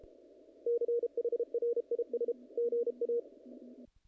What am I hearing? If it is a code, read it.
Morse code, heard as a keyed tone, about 33 wpm, 470 Hz. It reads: CHRIS GA
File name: contests / naqp-cw-k5tr-op-k5pi- -1800